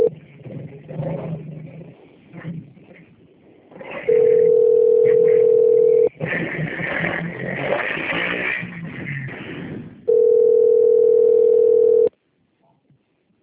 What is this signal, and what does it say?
Ringback tone (North America), 440 + 480 Hz.